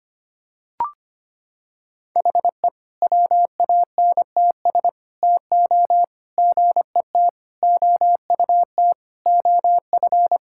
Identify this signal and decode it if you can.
Morse code, heard as a keyed tone, about 25 wpm, 700 Hz. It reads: HE WANTS TO GET OUT OF
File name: practice / 25wpm / Sentences with 4-Word Phrases - 25wpm